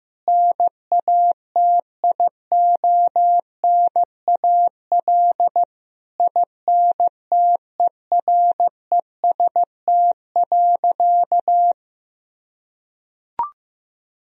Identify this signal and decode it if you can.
Morse code, heard as a keyed tone, about 15 wpm, 700 Hz. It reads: NATIONAL INTEREST.
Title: Sentences from Top 1000 Words - Review Part 4 - 15wpm